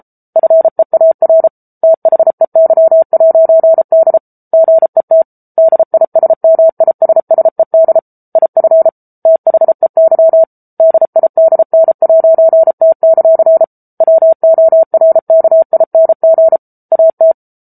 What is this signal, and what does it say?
Morse code, heard as a keyed tone, about 34 wpm, 650 Hz. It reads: FEAR THEY'D GET DISMISSED IF THEY DIDN'T; WORKING AT
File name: Book Two / 07-Chapter 7 - 33 WPM